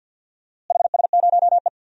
Morse code, heard as a keyed tone, about 52 wpm, 700 Hz. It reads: HS0E
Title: DXCC-100 Call Signs - 4 Characters - Repeated 3x - 50wpm